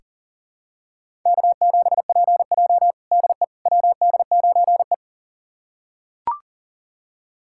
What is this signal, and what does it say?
Morse code, heard as a keyed tone, about 40 wpm, 700 Hz. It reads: K7PJ DE WD9E
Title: USA Callsign DE Callsign - 4 Characters - 40wpm